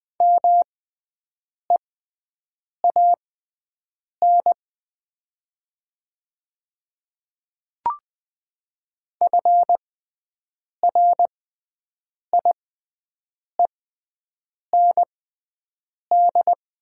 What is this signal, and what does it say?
Morse code, heard as a keyed tone, about 20 wpm, spaced as for 7 wpm, 700 Hz. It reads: MEAN FRIEND
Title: 6x Character Spacing - Top 300 Words - 20wpm